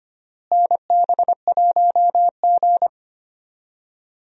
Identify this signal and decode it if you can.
Morse code, heard as a keyed tone, about 25 wpm, 700 Hz. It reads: NB1G
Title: USA Call Signs - 4 Characters - Repeated 3x - 25wpm